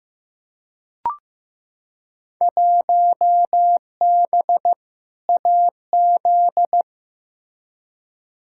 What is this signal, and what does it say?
Morse code, heard as a keyed tone, about 15 wpm, 700 Hz. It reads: 1B AZ
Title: Field Day Exchanges Encore - 15wpm